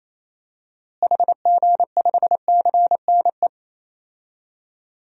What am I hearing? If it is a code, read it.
Morse code, heard as a keyed tone, about 28 wpm, 700 Hz. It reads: HG5CNE
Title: DXCC-100 Call Signs - 6 Characters - Repeated 2x - 28wpm